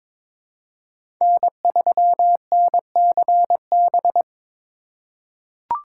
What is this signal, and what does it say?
Morse code, heard as a keyed tone, about 22 wpm, 700 Hz. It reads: N3NCB
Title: USA Call Signs - 5 Characters - 3x Character Spacing - 22wpm